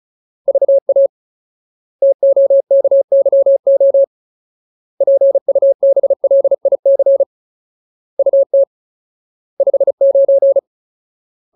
Morse code, heard as a keyed tone, about 34 wpm, 550 Hz. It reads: VA TOKYO PUBLIC UT 59
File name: Ultimate Word List / 35WPM Ultimate